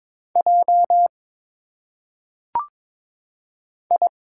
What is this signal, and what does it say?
Morse code, heard as a keyed tone, about 22 wpm, 700 Hz. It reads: J I
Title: Single Letters - Phonetic Pronunciation 22wpm